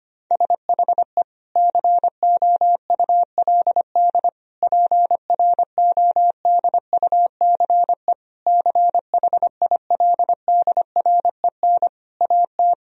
Morse code, heard as a keyed tone, about 25 wpm, 700 Hz. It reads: SHE COULD PRODUCE CHILDREN AT
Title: Sentences from Top 1000 Words - Review Part 3 - 25wpm